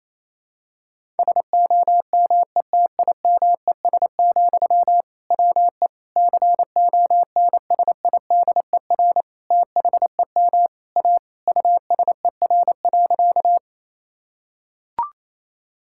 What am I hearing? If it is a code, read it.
Morse code, heard as a keyed tone, about 28 wpm, 700 Hz. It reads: SOMETIMES, WE CONSIDER THEM A USER.
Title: Sentences from Top 1000 Words - Review Part 5 - 28wpm